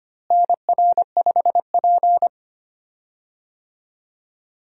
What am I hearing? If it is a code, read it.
Morse code, heard as a keyed tone, about 25 wpm, 700 Hz. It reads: NR5P
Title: USA Call Signs - 4 Characters - Repeated 3x - 25wpm